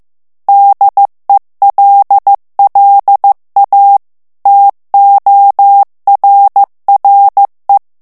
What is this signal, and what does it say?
Morse code, heard as a keyed tone, about 15 wpm, 800 Hz. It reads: DELLA TORRE
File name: ITALIEN texte clair 1 vitesse 14 mots minute